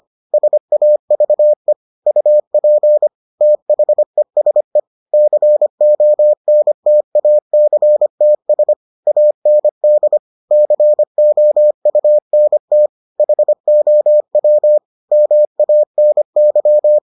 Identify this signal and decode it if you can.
Morse code, heard as a keyed tone, about 25 wpm, 600 Hz. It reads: SAVE UP THESE CONTACTS AND COUNT HOW MANY